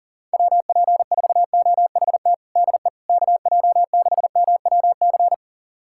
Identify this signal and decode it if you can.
Morse code, heard as a keyed tone, about 40 wpm, 700 Hz. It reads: WP4OHT DE KJ6MWC
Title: USA Callsign DE Callsign 40wpm